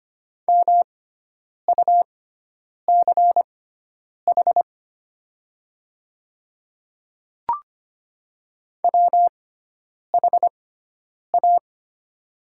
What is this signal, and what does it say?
Morse code, heard as a keyed tone, about 25 wpm, spaced as for 9 wpm, 690 Hz. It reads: MUCH WHA
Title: 6x Character Spacing - Top 300 Words - 25wpm